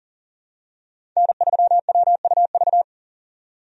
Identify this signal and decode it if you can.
Morse code, heard as a keyed tone, about 40 wpm, 700 Hz. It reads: N3WUV
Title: USA Call Signs - 5 Characters - 40wpm